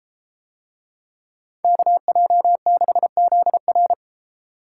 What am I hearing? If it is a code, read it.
Morse code, heard as a keyed tone, about 33 wpm, 700 Hz. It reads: KJ6ZR